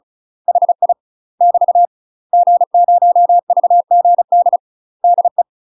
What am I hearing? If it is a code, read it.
Morse code, heard as a keyed tone, about 36 wpm, 700 Hz. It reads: HI <BT> G0VGD DE
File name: UK MASTER QSOs TRAINING / 35WPM / UK MASTER QSOs 0020